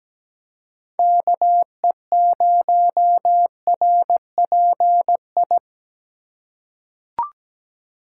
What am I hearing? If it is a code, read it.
Morse code, heard as a keyed tone, about 17 wpm, 700 Hz. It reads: KE0RPI
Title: USA Call Signs - 3x Character Spacing - 17wpm